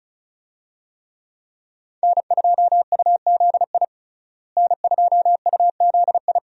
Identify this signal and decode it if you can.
Morse code, heard as a keyed tone, about 36 wpm, 700 Hz. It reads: N2UZI N2UZI
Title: USA Call Signs - 5 Characters - Repeated 2x - 35wpm